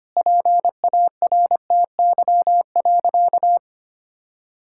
Morse code, heard as a keyed tone, about 25 wpm, 700 Hz. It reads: PARTY.
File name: practice / 25wpm / Difficult Sentences with Hints - 25wpm